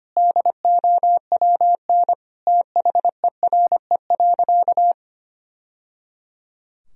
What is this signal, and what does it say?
Morse code, heard as a keyed tone, about 25 wpm, 700 Hz. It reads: DOWN THERE.
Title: Sentences from Top 1000 Words - Review Part 4 - 25wpm